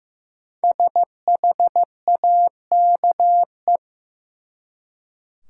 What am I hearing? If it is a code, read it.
Morse code, heard as a keyed tone, about 15 wpm, 700 Hz. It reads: SHAKE
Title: Five Letter Words - 15wpm